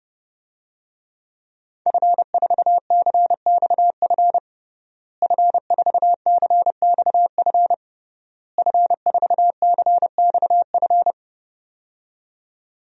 Morse code, heard as a keyed tone, about 30 wpm, 700 Hz. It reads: F4CXF F4CXF F4CXF
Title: DXCC-100 Call Signs - 5 Characters - Repeated 3x - 30wpm